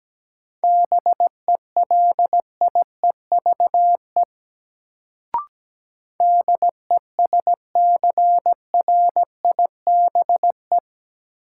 Morse code, heard as a keyed tone, about 17 wpm, 700 Hz. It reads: BELIEVE DESCRIBE
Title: Sentences from Top 1000 Words - New Word Drill - 17wpm